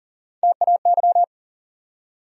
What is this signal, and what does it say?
Morse code, heard as a keyed tone, about 40 wpm, 700 Hz. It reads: TAY
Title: USA Call Sign Postfixes - 3 Letters 40wpm